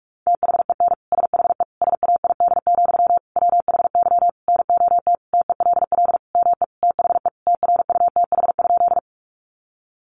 Morse code, heard as a keyed tone, about 44 wpm, 700 Hz. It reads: THEN SHE SAID, WHY NOT TELL ME THE TRUTH?
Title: Sentences from Top 800 Words 45wpm